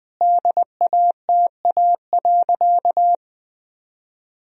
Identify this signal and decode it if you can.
Morse code, heard as a keyed tone, about 20 wpm, 700 Hz. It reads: DATA.
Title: Sentences from Top 1000 Words - Review Part 3 - 20wpm